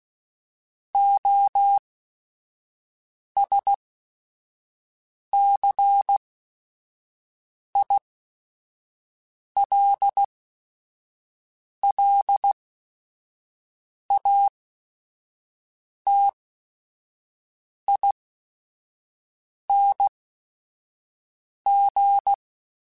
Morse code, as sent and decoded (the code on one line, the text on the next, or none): --- ... -.-. .. .-.. .-.. .- - .. -. --.
OSCILLATING